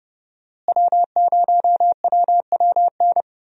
Morse code, heard as a keyed tone, about 30 wpm, 700 Hz. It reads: W0WWN
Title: USA Call Signs - 5 Characters - Repeated 3x - 30wpm